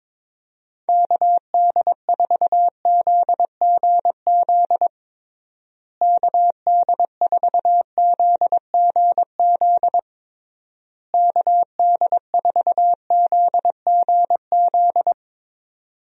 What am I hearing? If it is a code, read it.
Morse code, heard as a keyed tone, about 22 wpm, 700 Hz. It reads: KD4ZGZ KD4ZGZ KD4ZGZ
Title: USA Call Signs - 6 Characters - Repeated 3x - 22wpm